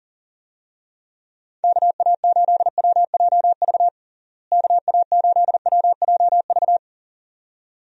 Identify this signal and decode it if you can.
Morse code, heard as a keyed tone, about 40 wpm, 700 Hz. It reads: KA8WJV KA8WJV
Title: USA Call Signs - 6 Characters - Repeated 2x - 40wpm